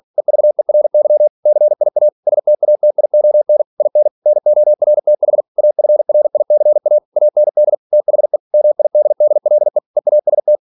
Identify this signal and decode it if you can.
Morse code, heard as a keyed tone, about 46 wpm, 600 Hz. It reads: EVERY CIA STATION IN NORTH AFRICA AND THE MIDDLE EAST
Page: Never 47wpm